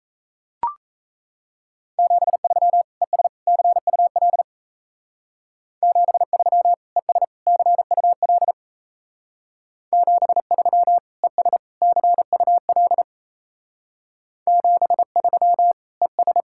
Morse code, heard as a keyed tone, about 33 wpm, 700 Hz. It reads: 73 ES CUL 73 ES CUL 73 ES CUL 73 ES
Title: Speed-Racing Common QSO Phrases - 28wpm